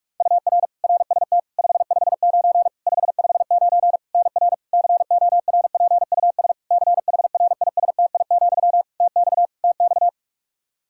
Morse code, heard as a keyed tone, about 46 wpm, 700 Hz. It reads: UR RST 559 559 NR CORPUS CHRISTI, TX TX
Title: K1USN SST QSO Practice 45wpm